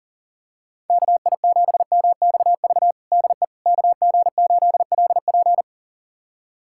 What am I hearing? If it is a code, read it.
Morse code, heard as a keyed tone, about 40 wpm, 700 Hz. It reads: KI7MXV DE KG8LP
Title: USA Callsign DE Callsign 40wpm